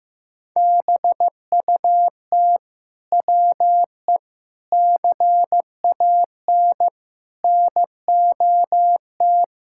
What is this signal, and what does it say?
Morse code, heard as a keyed tone, about 15 wpm, 700 Hz. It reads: BUT WE CAN NOT
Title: Sentences from Top 1000 Words - Review Part 5 - 15wpm